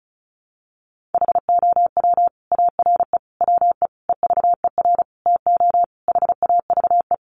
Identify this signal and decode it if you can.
Morse code, heard as a keyed tone, about 34 wpm, 700 Hz. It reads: HOW ARE WE EVER TO HAVE
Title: Sentences from Top 1500 Words 35wpm